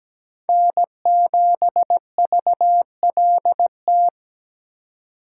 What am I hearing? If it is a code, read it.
Morse code, heard as a keyed tone, about 20 wpm, 700 Hz. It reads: N7VLT
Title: Speed Racing - USA 1x3 Call Signs 17wpm